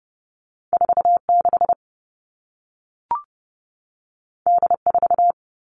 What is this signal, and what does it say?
Morse code, heard as a keyed tone, about 30 wpm, 700 Hz. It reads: 4 6 D 4
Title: Two Challenging Letter-Number Combinations 30wpm